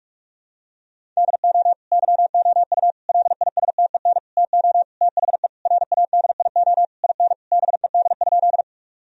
Morse code, heard as a keyed tone, about 46 wpm, 700 Hz. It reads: DO YOU LISTEN TO THE RADIO IN BED?
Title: Sentences from Top 1000 Words - Review Part 3 - 45wpm